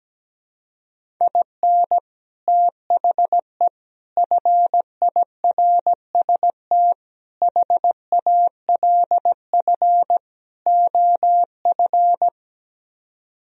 Morse code, heard as a keyed tone, about 17 wpm, 700 Hz. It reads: IN THE FIRST HALF OF
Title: Speed-Racing - Sets of 5 Words 17wpm